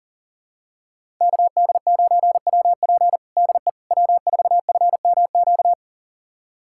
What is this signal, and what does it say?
Morse code, heard as a keyed tone, about 40 wpm, 700 Hz. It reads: KD9WP DE W4FMQ
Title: USA Callsign DE Callsign - 5 Characters - 40wpm